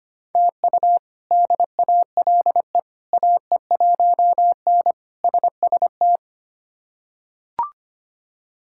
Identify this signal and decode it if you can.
Morse code, heard as a keyed tone, about 25 wpm, 700 Hz. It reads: TU DALE AE1N SST
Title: K1USN SST QSO Practice - 25wpm